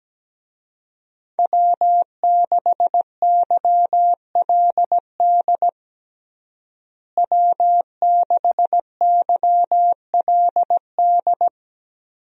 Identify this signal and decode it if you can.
Morse code, heard as a keyed tone, about 17 wpm, 700 Hz. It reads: W6YLD W6YLD
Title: USA Call Signs - 5 Characters - Repeated 3x - 17wpm